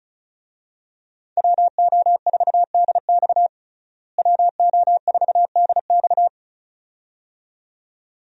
Morse code, heard as a keyed tone, about 36 wpm, 700 Hz. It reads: WO4DX WO4DX
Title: USA Call Signs - 5 Characters - Repeated 2x - 35wpm